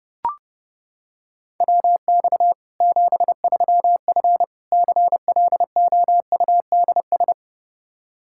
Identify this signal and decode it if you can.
Morse code, heard as a keyed tone, about 32 wpm, 700 Hz. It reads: WX 73F CLOUDS
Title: Practice QSO - Exchange 2 - Info - 30wpm